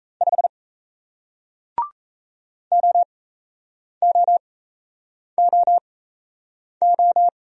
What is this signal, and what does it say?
Morse code, heard as a keyed tone, about 37 wpm, spaced as for 8 wpm, 700 Hz. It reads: 5 OOOO